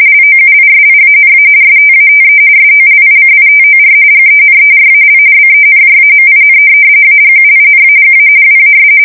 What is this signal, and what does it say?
RTTY, about 45.45 baud. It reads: LOUISIANA STATE SOLICITOR GENERAL BENJAMIN AGUINAGA WAS
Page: 170 Shift 2125/2295